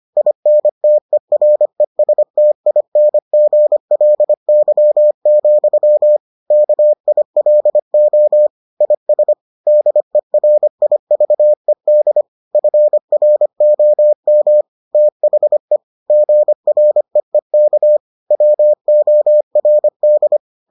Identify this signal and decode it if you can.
Morse code, heard as a keyed tone, about 25 wpm, 600 Hz. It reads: INTERESTINGLY, KILO IS DERIVED FROM THE GREEK WORD